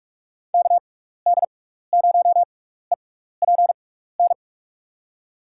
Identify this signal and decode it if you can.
Morse code, heard as a keyed tone, about 43 wpm, spaced as for 15 wpm, 700 Hz. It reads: KD0EPN
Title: USA Call Signs - 6 Characters - 6x Character Spacing - 45wpm